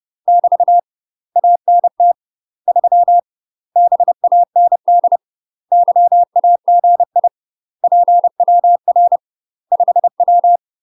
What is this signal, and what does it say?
Morse code, heard as a keyed tone, about 30 wpm, 700 Hz. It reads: <BT> ANT 3 BAND YAGI PWR 5W
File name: UK MASTER QSOs TRAINING / 30WPM / UK MASTER QSOs 0038